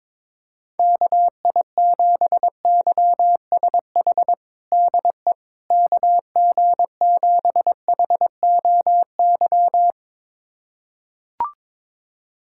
Morse code, heard as a keyed tone, about 22 wpm, 700 Hz. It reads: KI7YSH DE KG7HOY
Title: USA Callsign DE Callsign 22wpm